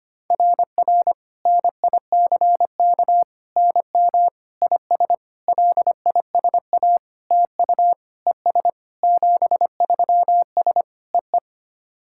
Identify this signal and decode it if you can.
Morse code, heard as a keyed tone, about 25 wpm, 700 Hz. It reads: RR NICK NM IS LISA TU ES 73S EE
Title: SOTA QSO Practice 25wpm